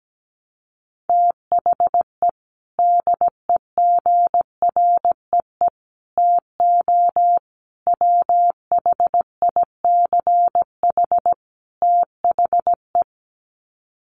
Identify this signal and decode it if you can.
Morse code, heard as a keyed tone, about 17 wpm, 700 Hz. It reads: THE DEGREE TO WHICH THE
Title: Sets of 5 Words Encore2 17wpm